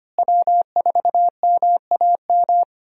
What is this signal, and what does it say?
Morse code, heard as a keyed tone, about 25 wpm, 700 Hz. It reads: W4MAM
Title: USA Call Signs - 5 Characters - Repeated 2x - 25wpm